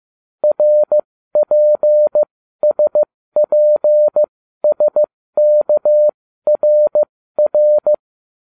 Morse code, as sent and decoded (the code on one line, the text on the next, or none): .-. .--. ... .--. ... -.- .-. .-.
RPSPSKRR